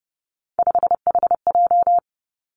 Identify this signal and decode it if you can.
Morse code, heard as a keyed tone, about 30 wpm, 700 Hz. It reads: 5HJ
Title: Three Challenging Letter-Number Combinations 30wpm